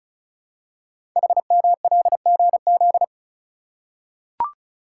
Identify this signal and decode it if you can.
Morse code, heard as a keyed tone, about 36 wpm, 700 Hz. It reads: HMLGZ
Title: Code Groups - 5 Letters - 35wpm